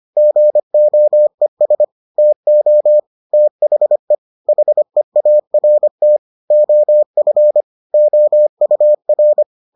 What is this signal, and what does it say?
Morse code, heard as a keyed tone, about 25 wpm, 600 Hz. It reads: GOES TO THE HEART OF OUR